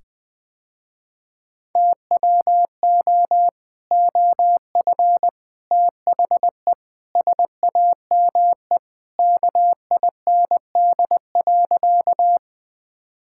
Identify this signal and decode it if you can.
Morse code, heard as a keyed tone, about 20 wpm, 700 Hz. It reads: TWO OF THE SAME KIND.
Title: Sentences with 3-Word Phrases - Spoken Beginning - 20wpm